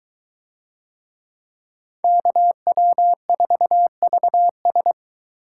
Morse code, heard as a keyed tone, about 22 wpm, 700 Hz. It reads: KW4VS